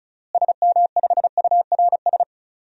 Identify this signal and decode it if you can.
Morse code, heard as a keyed tone, about 35 wpm, 700 Hz. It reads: SM5URS
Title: DXCC-100 Call Signs - 6 Characters - Repeated 3x - 35wpm